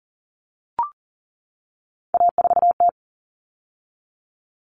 Morse code, heard as a keyed tone, about 40 wpm, 700 Hz. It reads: A4T
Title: All Cut Numbers - 3-Digit 40wpm